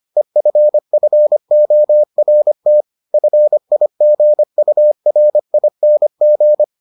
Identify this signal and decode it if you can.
Morse code, heard as a keyed tone, about 25 wpm, 600 Hz. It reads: EFFORT FIGURING